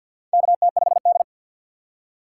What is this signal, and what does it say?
Morse code, heard as a keyed tone, about 51 wpm, 700 Hz. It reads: KT5D
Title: USA Call Signs - 4 Characters - 6x Character Spacing - 50wpm